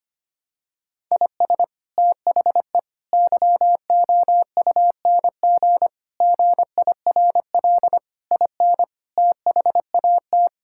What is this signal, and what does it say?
Morse code, heard as a keyed tone, about 25 wpm, 700 Hz. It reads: IS THE YOUNG GIRL IN THAT
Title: Sentences from Top 1000 Words - Review Part 5 - 25wpm